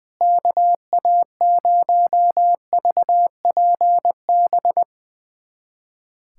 Morse code, heard as a keyed tone, about 21 wpm, 700 Hz. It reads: KA0VPB
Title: USA Callsign DE Callsign - 6 Characters - 20wpm